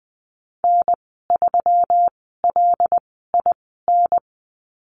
Morse code, heard as a keyed tone, about 20 wpm, spaced as for 14 wpm, 700 Hz. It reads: N3LIN
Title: 173 Lesson 28g - Number 9 - Call Signs 20wpm